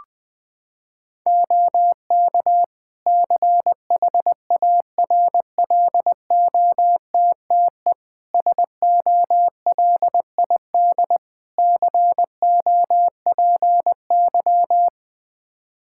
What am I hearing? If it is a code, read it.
Morse code, heard as a keyed tone, about 20 wpm, 700 Hz. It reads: OK CHARLOTTE SOLID COPY
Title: Practice QSO - Exchange 3 - More Info - 20wpm